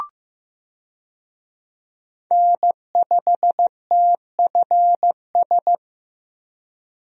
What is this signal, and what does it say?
Morse code, heard as a keyed tone, about 16 wpm, 700 Hz. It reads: N5TFS
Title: USA Call Signs - 5 Characters - Repeated 2x - 15wpm